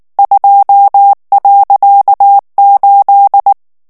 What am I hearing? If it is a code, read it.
Morse code, heard as a keyed tone, about 19 wpm, 800 Hz. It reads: 2.8
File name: ANGLAIS texte clair 3 vitesse 18 mots minute